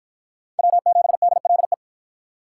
Morse code, heard as a keyed tone, about 48 wpm, 700 Hz. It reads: W7DLE